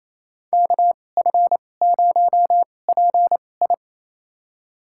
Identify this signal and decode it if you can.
Morse code, heard as a keyed tone, about 28 wpm, spaced as for 20 wpm, 700 Hz. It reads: KF0PI
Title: USA Call Signs - 5 Characters - 2x Character Spacing - 28wpm